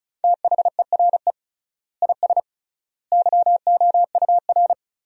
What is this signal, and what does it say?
Morse code, heard as a keyed tone, about 34 wpm, 700 Hz. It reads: THERE IS YOUR
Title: 3x Word-Spacing - Sentences from Top 500 Words - 35wpm